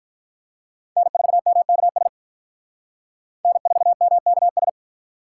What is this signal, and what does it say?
Morse code, heard as a keyed tone, about 45 wpm, 700 Hz. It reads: N4MKS N4MKS